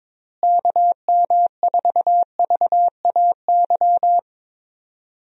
Morse code, heard as a keyed tone, about 22 wpm, 700 Hz. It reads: KM4VAY
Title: USA Call Signs - 6 Characters - Repeated 2x - 22wpm